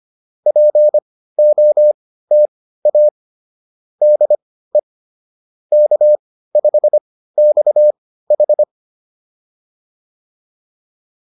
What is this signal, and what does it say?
Morse code, heard as a keyed tone, about 25 wpm, spaced as for 15 wpm, 600 Hz. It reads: POTA DE K5XH